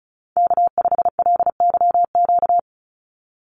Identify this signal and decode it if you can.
Morse code, heard as a keyed tone, about 34 wpm, 700 Hz. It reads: K5LYQ
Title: USA Call Signs 35wpm